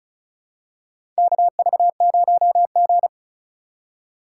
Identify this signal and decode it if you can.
Morse code, heard as a keyed tone, about 34 wpm, 700 Hz. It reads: KV0G
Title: USA Call Signs - 4 Characters - 4x Character Spacing - 35wpm